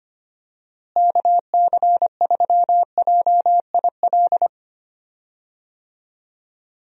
Morse code, heard as a keyed tone, about 25 wpm, 700 Hz. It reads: KC3JIL
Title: USA Call Signs - 6 Characters - Repeated 3x - 25wpm